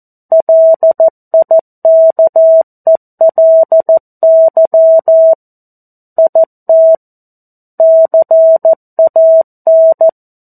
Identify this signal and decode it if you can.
Morse code, heard as a keyed tone, about 14 wpm, 650 Hz. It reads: LIKELY IT CAN
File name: 02-Chapter-02 14 WPM